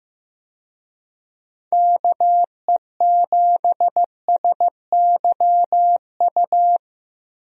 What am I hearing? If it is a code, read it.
Morse code, heard as a keyed tone, about 15 wpm, 700 Hz. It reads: KE7SYU
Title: USA Call Signs - 6 Characters - Repeated 2x - 15wpm